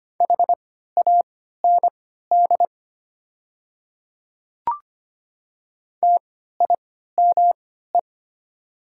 Morse code, heard as a keyed tone, about 25 wpm, spaced as for 14 wpm, 700 Hz. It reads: HAND TIME
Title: 3x Character Spacing - Top 500 Words - 25wpm